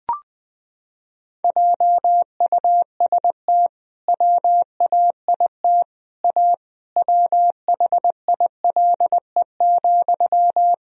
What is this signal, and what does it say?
Morse code, heard as a keyed tone, about 20 wpm, 700 Hz. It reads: JUST WAIT A WHILE,
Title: Sentences from Top 1000 Words - Review Part 1 - 20wpm